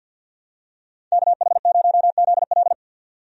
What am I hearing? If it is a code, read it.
Morse code, heard as a keyed tone, about 46 wpm, 700 Hz. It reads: KH0ZL
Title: USA Call Signs - 5 Characters - 50wpm